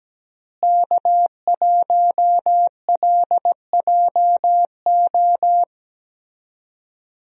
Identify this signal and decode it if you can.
Morse code, heard as a keyed tone, about 17 wpm, 700 Hz. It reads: K1LJO
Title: USA Call Signs - 5 Characters - Repeated 2x - 17wpm